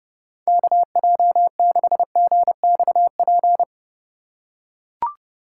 Morse code, heard as a keyed tone, about 30 wpm, 700 Hz. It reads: KJ6GXP